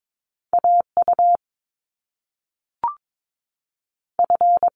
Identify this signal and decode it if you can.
Morse code, heard as a keyed tone, about 22 wpm, 700 Hz. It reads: AU F
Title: Top Two Letter Combinations in Words 22wpm